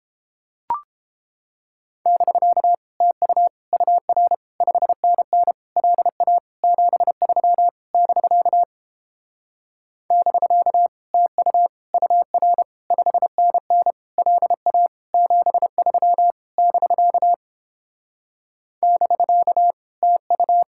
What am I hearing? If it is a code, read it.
Morse code, heard as a keyed tone, about 29 wpm, 700 Hz. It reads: <BK> TU UR 5NN LA 73 <BK> <BK> TU UR 5NN LA 73 <BK> <BK> TU